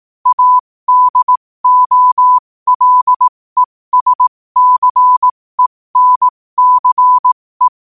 Morse code, heard as a keyed tone, about 18 wpm, spaced as for 15 wpm, 1000 Hz. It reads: ADOLESCENCE
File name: qotdfiles / 15wpm